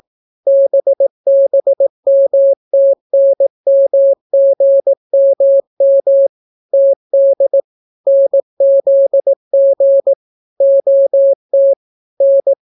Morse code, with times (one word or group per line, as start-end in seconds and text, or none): 0.5-6.3 BBMTNMGMM
6.7-7.6 TD
8.1-10.1 NZG
10.6-11.7 OT
12.2-12.5 N